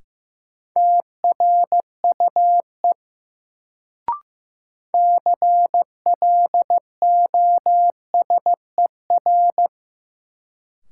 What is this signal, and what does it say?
Morse code, heard as a keyed tone, about 15 wpm, 700 Hz. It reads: TRUE CLOSER
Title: Sets of 5 Words Encore - New Word Drill - 15wpm